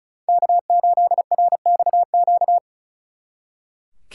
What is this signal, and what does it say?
Morse code, heard as a keyed tone, about 36 wpm, 700 Hz. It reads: K8RXQ
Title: Speed Racing - USA 1x3 Call Signs 35wpm